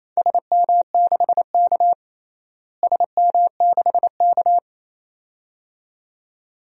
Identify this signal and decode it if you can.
Morse code, heard as a keyed tone, about 28 wpm, 700 Hz. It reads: SM6K SM6K
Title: DXCC-100 Call Signs - 4 Characters - Repeated 2x - 28wpm